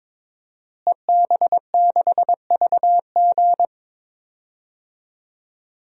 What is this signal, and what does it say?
Morse code, heard as a keyed tone, about 22 wpm, 700 Hz. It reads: EB6VG
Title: DXCC-100 Call Signs - 5 Characters - Repeated 2x - 22wpm